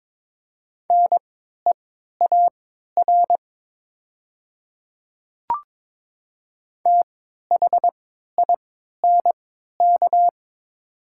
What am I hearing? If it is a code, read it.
Morse code, heard as a keyed tone, about 22 wpm, spaced as for 12 wpm, 700 Hz. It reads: NEAR THINK
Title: 3x Character Spacing - Top 300 Words - 22wpm